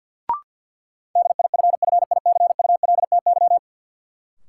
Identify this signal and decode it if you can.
Morse code, heard as a keyed tone, about 52 wpm, 700 Hz. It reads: DIFFICULTY
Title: Sentences from Top 1500 Words - New Word Drill Alt - 50wpm